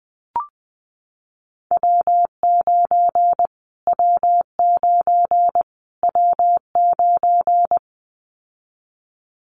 Morse code, heard as a keyed tone, about 20 wpm, 700 Hz. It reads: W9 W9 W9
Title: Callsign Prefixes for Top 25 DXCC - Repeated 3x - 20wpm